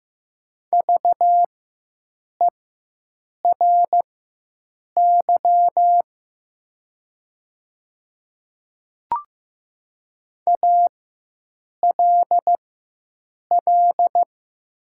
Morse code, heard as a keyed tone, about 15 wpm, spaced as for 7 wpm, 700 Hz. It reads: VERY ALL
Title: 4x Character Spacing - Top 200 Words - 15wpm